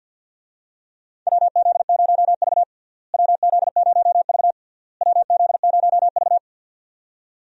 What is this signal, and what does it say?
Morse code, heard as a keyed tone, about 49 wpm, 700 Hz. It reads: WZ0V WZ0V WZ0V
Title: USA Call Signs - 4 Characters - Repeated 3x - 50wpm